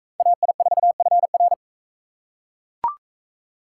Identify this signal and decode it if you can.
Morse code, heard as a keyed tone, about 40 wpm, 700 Hz. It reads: AI4FR